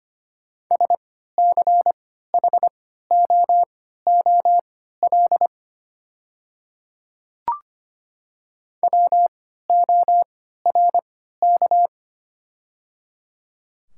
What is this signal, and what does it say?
Morse code, heard as a keyed tone, about 25 wpm, spaced as for 14 wpm, 700 Hz. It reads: SCHOOL WORK
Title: 3x Character Spacing - Top 300 Words - 25wpm